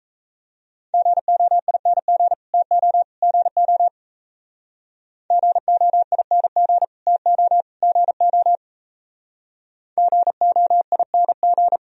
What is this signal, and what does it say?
Morse code, heard as a keyed tone, about 40 wpm, 700 Hz. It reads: GOING TO GO GOING TO GO GOING